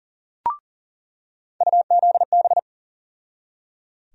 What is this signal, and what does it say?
Morse code, heard as a keyed tone, about 38 wpm, 700 Hz. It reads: UZB
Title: USA Call Sign Postfixes - 3 Letters 40wpm